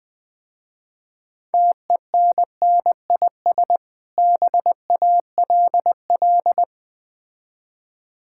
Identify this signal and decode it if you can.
Morse code, heard as a keyed tone, about 20 wpm, 700 Hz. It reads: TENNIS BALL
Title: Easy Words - Speed-Racing with Hints - 20wpm